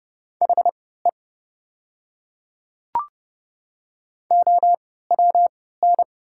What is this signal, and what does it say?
Morse code, heard as a keyed tone, about 30 wpm, spaced as for 17 wpm, 690 Hz. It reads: HE OWN
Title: 3x Character Spacing - Top 500 Words - 30wpm